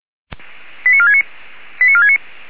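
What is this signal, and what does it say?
Five-tone selective call (ZVEI), heard as quick runs of five tones, tones of about 80 ms. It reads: 89469 89469
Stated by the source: ZVEI five-tone selective call received with an SDR.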